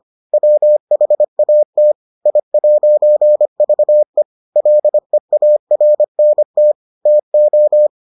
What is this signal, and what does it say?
Morse code, heard as a keyed tone, about 25 wpm, 600 Hz. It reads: WHAT I'VE LEARNT TO